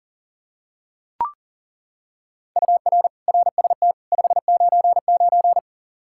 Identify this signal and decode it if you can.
Morse code, heard as a keyed tone, about 40 wpm, 700 Hz. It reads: UR RST 599
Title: Practice QSO - Exchange 1 - Beginning - 40wpm